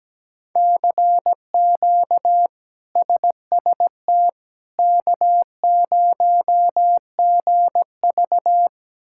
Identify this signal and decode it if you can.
Morse code, heard as a keyed tone, about 17 wpm, 700 Hz. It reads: CQ SST K0GV
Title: K1USN SST QSO Practice - 17wpm